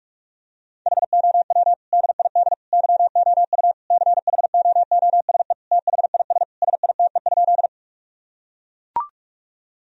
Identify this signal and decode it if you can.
Morse code, heard as a keyed tone, about 41 wpm, 700 Hz. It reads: HOW DID YOU CHOOSE THIS SITE?
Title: Sentences from Top 1000 Words - Review Part 4 - 45wpm